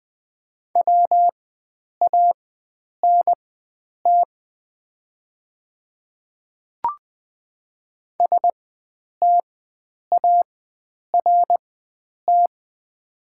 Morse code, heard as a keyed tone, about 20 wpm, spaced as for 9 wpm, 700 Hz. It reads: WANT START
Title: 4x Character Spacing - Top 300 Words - 20wpm